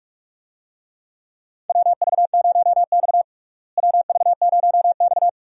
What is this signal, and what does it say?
Morse code, heard as a keyed tone, about 45 wpm, 700 Hz. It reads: WV0X WV0X
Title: USA Call Signs - 4 Characters - Repeated 2x - 45wpm